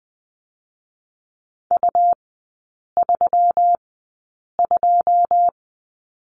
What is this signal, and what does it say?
Morse code, heard as a keyed tone, about 20 wpm, spaced as for 8 wpm, 700 Hz. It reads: U32
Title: 218 Lesson 35e - Number 0 - Three Characters 20wpm